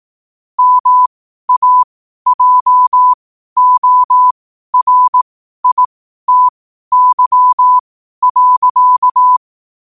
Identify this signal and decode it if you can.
Morse code, heard as a keyed tone, about 18 wpm, spaced as for 12 wpm, 1000 Hz. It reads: MAJORITY.
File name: qotdfiles / 12wpm